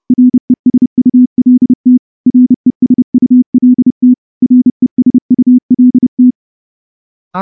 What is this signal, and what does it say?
Morse code, heard as a keyed tone, about 28 wpm, 250 Hz. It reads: RESULT RESULT RESULT
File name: morse-training-30WPM